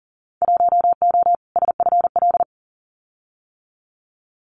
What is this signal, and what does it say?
Morse code, heard as a keyed tone, about 40 wpm, 700 Hz. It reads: 1O SFL
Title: Winter Field Day Exchanges 40wpm